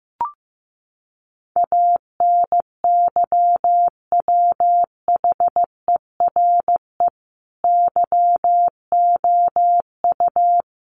Morse code, heard as a keyed tone, about 15 wpm, 700 Hz. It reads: ANYWHERE YOU
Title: Sentences from Top 1400 Words 15wpm